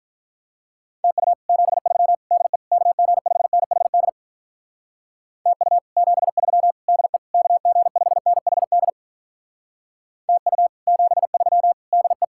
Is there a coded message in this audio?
Yes: Morse code from 1.0 to 12.2 s, heard as a keyed tone, about 44 wpm, 700 Hz. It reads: TU 73 DE KG5NHD TU 73 DE KG5NHD TU 73 DE